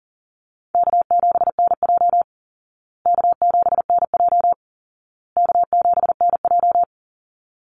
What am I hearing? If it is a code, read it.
Morse code, heard as a keyed tone, about 40 wpm, 700 Hz. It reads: K7NJ K7NJ K7NJ
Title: Easy Call Signs Repeated 3x - 40wpm